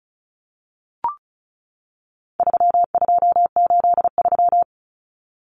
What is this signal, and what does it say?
Morse code, heard as a keyed tone, about 34 wpm, 700 Hz. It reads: 3283
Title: Common Cut Numbers - 4-Digit 35wpm